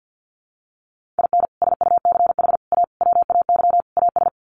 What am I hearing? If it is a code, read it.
Morse code, heard as a keyed tone, about 50 wpm, 700 Hz. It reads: IN SUCH A WAY AS
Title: Sets of 5 Words Encore 50wpm